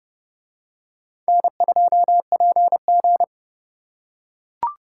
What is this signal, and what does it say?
Morse code, heard as a keyed tone, about 30 wpm, 700 Hz. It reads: N2PG